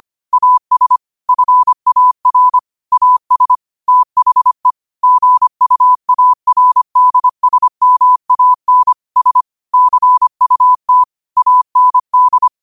Morse code, heard as a keyed tone, about 25 wpm, 1000 Hz. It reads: AS FAR AS THE GUARDSMAN S CUT AND